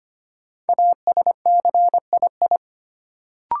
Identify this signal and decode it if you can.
Morse code, heard as a keyed tone, about 25 wpm, 700 Hz. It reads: ASCII
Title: Sets of 5 Words - New Word Drill - 25wpm